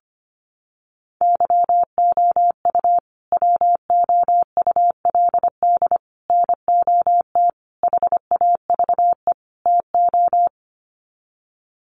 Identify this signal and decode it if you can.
Morse code, heard as a keyed tone, about 25 wpm, 700 Hz. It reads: YOU WOULD NOT HAVE TO
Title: Sets of 5 Words Encore 25wpm